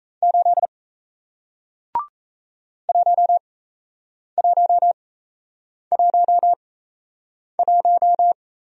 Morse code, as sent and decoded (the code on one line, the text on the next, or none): ---.. / .---- .---- .---- .----
8 1111